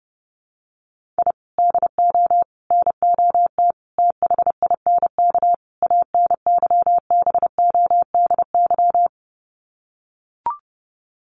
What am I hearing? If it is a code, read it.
Morse code, heard as a keyed tone, about 28 wpm, 700 Hz. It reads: I DO NOT THINK ANYBODY
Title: Sets of 5 Words 30wpm